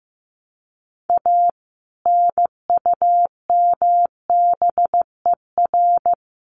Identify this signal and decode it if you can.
Morse code, heard as a keyed tone, about 15 wpm, 700 Hz. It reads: A NUMBER
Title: Sets of 3 Words 15wpm